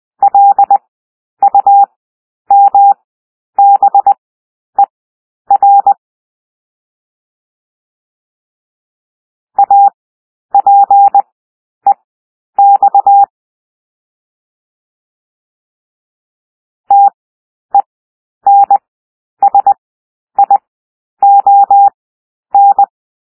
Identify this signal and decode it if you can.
Morse code, heard as a keyed tone, about 20 wpm, spaced as for 10 wpm, 800 Hz. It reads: LUMBER APEX TENSION
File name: cwmp3 / W1 SOUND WORDS 20 wpm